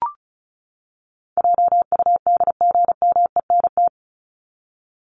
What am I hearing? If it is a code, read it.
Morse code, heard as a keyed tone, about 34 wpm, 700 Hz. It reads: JUDGMENT
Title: Top 1751-2000 Words 35wpm